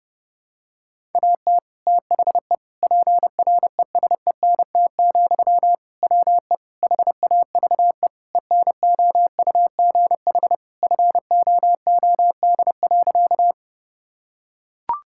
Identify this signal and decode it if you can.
Morse code, heard as a keyed tone, about 30 wpm, 700 Hz. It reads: AT THE PRESENT, WE HAVE ENOUGH FOOD.
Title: Sentences from Top 1000 Words - Review Part 5 - 30wpm